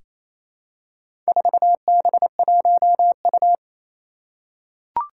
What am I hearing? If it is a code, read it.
Morse code, heard as a keyed tone, about 28 wpm, 700 Hz. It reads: 4B1U
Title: Four Challenging Letter-Number Combinations 28wpm